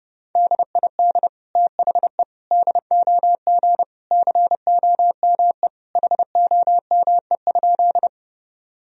Morse code, heard as a keyed tone, about 30 wpm, 700 Hz. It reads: DID THE DOG COME HOME?
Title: Sentences from Top 1000 Words - Review Part 2 - 30wpm